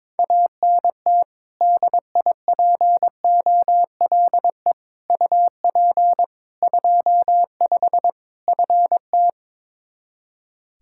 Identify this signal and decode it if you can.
Morse code, heard as a keyed tone, about 22 wpm, 700 Hz. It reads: ANT DIPOLE UP 25 FT
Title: Practice QSO - Exchange 2 - Info - 22wpm